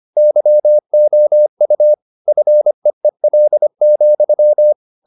Morse code, heard as a keyed tone, about 25 wpm, 600 Hz. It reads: YOU FEEL,